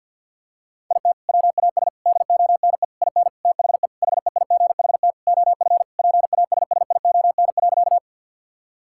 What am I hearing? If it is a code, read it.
Morse code, heard as a keyed tone, about 50 wpm, 700 Hz. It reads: IT WAS DONE IN THE HEIGHT OF PASSION.
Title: Sentences from Top 1600 Words 50wpm